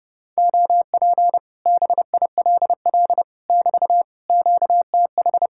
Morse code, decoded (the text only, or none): OP BILL <BT> QTH